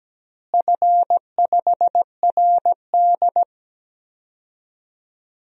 Morse code, heard as a keyed tone, about 17 wpm, 700 Hz. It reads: F5RD
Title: DXCC-100 Call Signs - 4 Characters - Repeated 2x - 17wpm